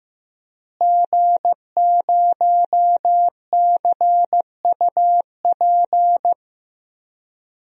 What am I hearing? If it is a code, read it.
Morse code, heard as a keyed tone, about 15 wpm, 700 Hz. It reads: G0CUP
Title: DXCC-100 Call Signs - 5 Characters - Repeated 3x - 15wpm